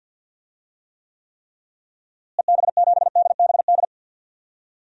Morse code, heard as a keyed tone, about 47 wpm, 700 Hz. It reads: EB7DBD
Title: DXCC-100 Call Signs - 6 Characters - Repeated 3x - 50wpm